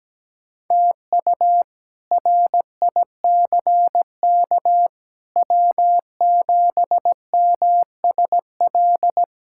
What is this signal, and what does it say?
Morse code, heard as a keyed tone, about 17 wpm, 700 Hz. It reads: TU RICK W7MSL
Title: K1USN SST QSO Practice - 17wpm